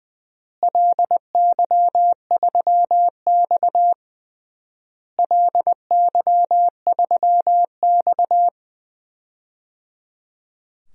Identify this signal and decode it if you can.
Morse code, heard as a keyed tone, about 20 wpm, 700 Hz. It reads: LY3X LY3X
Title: DXCC-100 Call Signs - 4 Characters - Repeated 3x - 20wpm